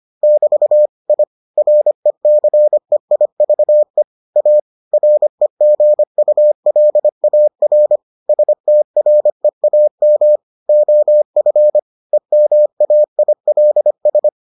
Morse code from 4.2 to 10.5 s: A REGULAR STREAM

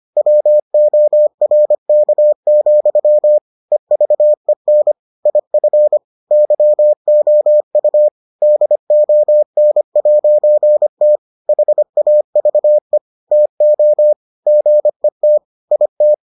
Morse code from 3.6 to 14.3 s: EVEN IF YOU DON'T HAVE TO

